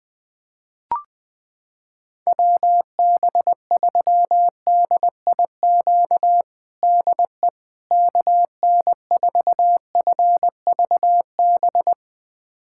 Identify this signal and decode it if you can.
Morse code, heard as a keyed tone, about 20 wpm, 700 Hz. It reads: WB3DIQ DE KN4FVB
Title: USA Callsign DE Callsign - 6 Characters - 20wpm